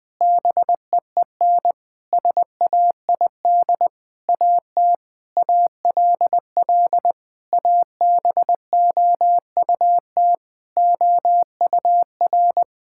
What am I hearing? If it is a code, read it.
Morse code, heard as a keyed tone, about 20 wpm, 700 Hz. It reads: BEEN SAID AT ALL ABOUT OUR
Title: Sentences from Top 1000 Words - Review Part 5 - 20wpm